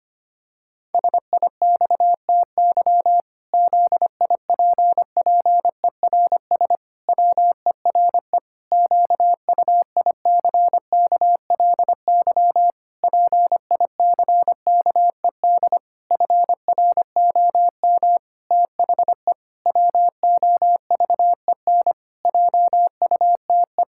Morse code, heard as a keyed tone, about 25 wpm, 700 Hz. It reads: SIXTY ZIPPERS WERE QUICKLY PICKED FROM THE WOVEN JUTE
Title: Pangram Challenge v2 - 25wpm